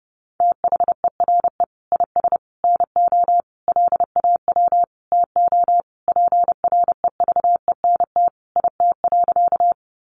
Morse code, heard as a keyed tone, about 30 wpm, 700 Hz. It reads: THERE IS NO LAW TO PREVENT IT.
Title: Sentences from Top 1100 Words 30wpm